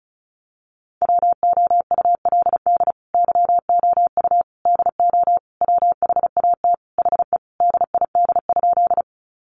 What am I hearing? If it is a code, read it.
Morse code, heard as a keyed tone, about 34 wpm, 700 Hz. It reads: WOULD YOU DO WHAT HE DID?
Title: Sentences from Top 100 Words 35wpm